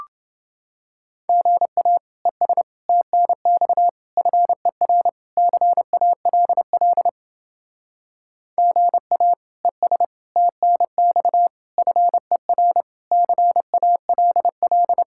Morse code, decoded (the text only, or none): GA ES TNX FER CALL GA ES TNX FER CALL